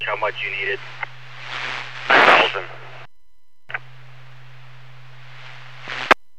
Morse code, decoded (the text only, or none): none